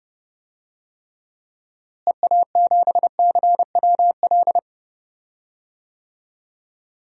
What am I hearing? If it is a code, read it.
Morse code, heard as a keyed tone, about 32 wpm, 700 Hz. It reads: EA7CWL
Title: DXCC-100 Call Signs - 6 Characters - Repeated 3x - 30wpm